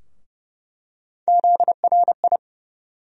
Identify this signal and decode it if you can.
Morse code, heard as a keyed tone, about 30 wpm, 700 Hz. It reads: ZRI
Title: USA Call Sign Postfixes - 3 Letters 30wpm